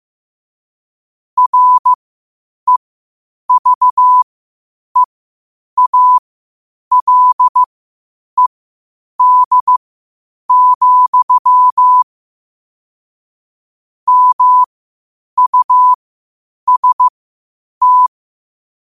Morse code, heard as a keyed tone, about 15 wpm, spaced as for 8 wpm, 1000 Hz. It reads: REVEALED, MUST
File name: qotdfiles / 8wpm